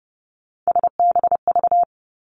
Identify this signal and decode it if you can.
Morse code, heard as a keyed tone, about 30 wpm, 700 Hz. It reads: SBV
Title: Three Challenging Letter-Number Combinations 30wpm